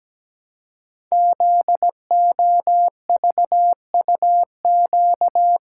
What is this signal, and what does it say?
Morse code, heard as a keyed tone, about 17 wpm, 700 Hz. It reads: ZOVUQ
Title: Code Groups - 5 Letters - 17wpm